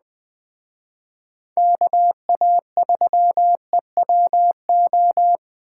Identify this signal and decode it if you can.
Morse code, heard as a keyed tone, about 21 wpm, 700 Hz. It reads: KA3EWO